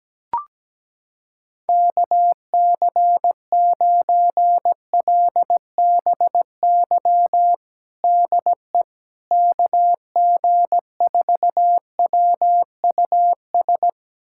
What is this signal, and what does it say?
Morse code, heard as a keyed tone, about 17 wpm, 700 Hz. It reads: KC9LBY DE KG4WUS
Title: USA Callsign DE Callsign - 6 Characters - 17wpm